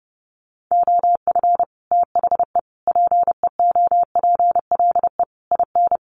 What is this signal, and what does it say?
Morse code, heard as a keyed tone, about 28 wpm, 700 Hz. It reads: OF THE PEOPLE IN
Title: Sets of 5 Words Encore 30wpm